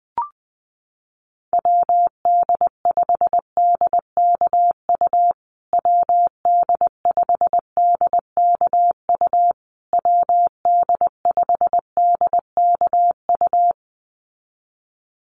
Morse code, heard as a keyed tone, about 20 wpm, 700 Hz. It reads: WD5DKU WD5DKU WD5DKU
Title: USA Call Signs Repeated 3x 20wpm